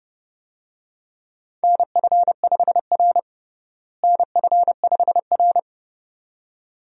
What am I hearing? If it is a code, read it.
Morse code, heard as a keyed tone, about 32 wpm, 700 Hz. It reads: NF5R NF5R
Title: USA Call Signs - 4 Characters - Repeated 2x - 30wpm